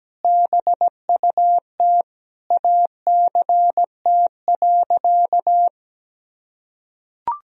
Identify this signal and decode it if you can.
Morse code, heard as a keyed tone, about 17 wpm, 700 Hz. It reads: BUT ACT.
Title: Sentences from Top 1000 Words - Review Part 1 - 17wpm